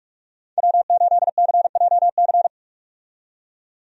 Morse code, heard as a keyed tone, about 46 wpm, 700 Hz. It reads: W8CJC
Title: USA Call Signs - 5 Characters - 45wpm